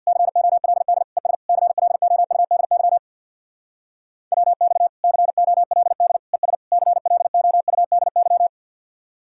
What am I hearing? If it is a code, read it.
Morse code, heard as a keyed tone, about 61 wpm, 700 Hz. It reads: COLD ES CLOUDY WX COLD ES CLOUDY
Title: Speed-Racing Common QSO Phrases - 50wpm